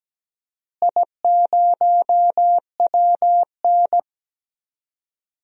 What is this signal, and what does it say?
Morse code, heard as a keyed tone, about 17 wpm, 700 Hz. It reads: I0WN
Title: DXCC-100 Call Signs - 4 Characters - Repeated 3x - 17wpm